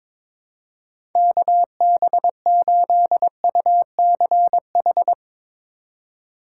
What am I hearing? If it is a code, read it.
Morse code, heard as a keyed tone, about 22 wpm, 700 Hz. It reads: KB8UCH
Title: USA Call Signs - 4x Character Spacing - 22wpm